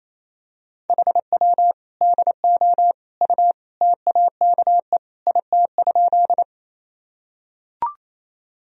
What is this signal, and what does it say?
Morse code, heard as a keyed tone, about 28 wpm, 700 Hz. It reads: HW DO U TAKE IT?
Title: Sentences from Top 200 Words and Abbreviations 28wpm